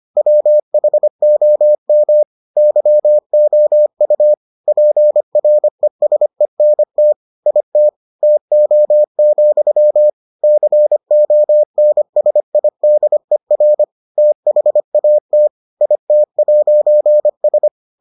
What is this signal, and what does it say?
Morse code, heard as a keyed tone, about 25 wpm, 600 Hz. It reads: WHOM YOU PRESENT IT TO, CONSIDER THAT IT'S